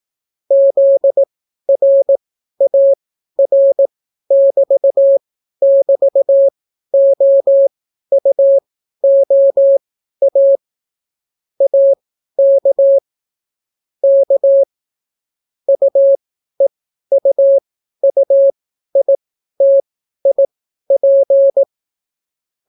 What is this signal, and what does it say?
Morse code, heard as a keyed tone, about 18 wpm, spaced as for 12 wpm, 550 Hz. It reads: ZRAR<BT><BT>OUOA AK K UEUUITIP